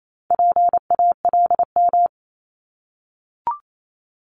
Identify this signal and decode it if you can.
Morse code, heard as a keyed tone, about 28 wpm, 700 Hz. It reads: PALM
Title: Multi-letter Phonemes Repeated 3x 28wpm